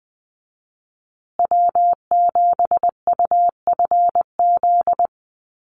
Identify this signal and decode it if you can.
Morse code, heard as a keyed tone, about 20 wpm, 700 Hz. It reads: W7UFZ
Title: USA Call Signs Repeated 3x 20wpm